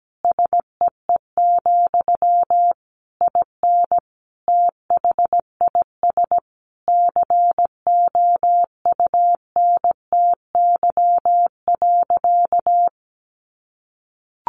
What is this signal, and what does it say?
Morse code, heard as a keyed tone, about 17 wpm, 700 Hz. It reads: SEE, IN THIS COUNTY.
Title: Sentences from Top 1200 Words 17wpm